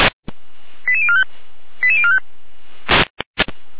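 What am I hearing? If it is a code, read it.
Five-tone selective call (ZVEI), heard as quick runs of five tones, tones of about 70 ms. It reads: 88045 88045
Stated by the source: ZVEI five-tone selective call received on an SDR.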